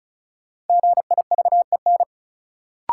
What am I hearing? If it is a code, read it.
Morse code, heard as a keyed tone, about 36 wpm, 700 Hz. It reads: GIVEN
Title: Sentences from Top 1400 Words - New Word Drill - 35wpm